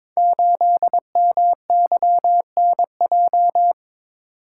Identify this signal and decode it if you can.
Morse code, heard as a keyed tone, about 22 wpm, 690 Hz. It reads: 8MYNJ
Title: Code Groups - 5 Letter:Numbers - 22wpm